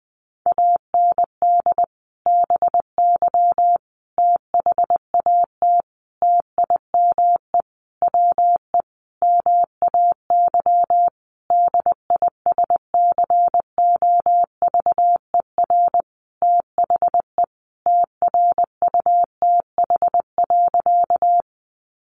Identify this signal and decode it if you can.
Morse code, heard as a keyed tone, about 20 wpm, 700 Hz. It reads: AND BY THAT TIME WE MAY DISCOVER THE TRUTH.
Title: Sentences from Top 1100 Words 20wpm